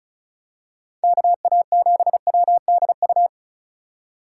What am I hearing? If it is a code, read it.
Morse code, heard as a keyed tone, about 34 wpm, 700 Hz. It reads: KA7WDU
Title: USA Call Signs - 6 Characters - 2x Character Spacing - 35wpm